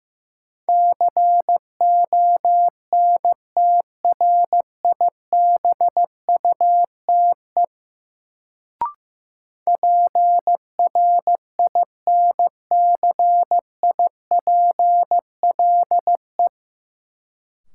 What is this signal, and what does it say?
Morse code, heard as a keyed tone, about 15 wpm, 700 Hz. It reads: CONTRIBUTE PRINCIPLE
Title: Sentences from Top 1400 Words - New Word Drill - 15wpm